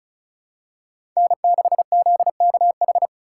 Morse code, heard as a keyed tone, about 34 wpm, 690 Hz. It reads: N6ZKH
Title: USA Call Signs - 3x Character Spacing - 35wpm